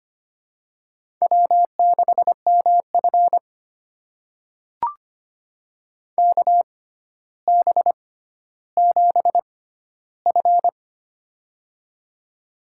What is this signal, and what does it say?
Morse code, heard as a keyed tone, about 25 wpm, spaced as for 13 wpm, 700 Hz. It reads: W6MF KB7F
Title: USA Call Signs - 6x Character Spacing - 25wpm